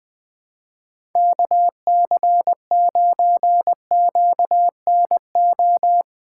Morse code, heard as a keyed tone, about 20 wpm, 700 Hz. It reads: KC9QNO
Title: USA Callsign DE Callsign 20wpm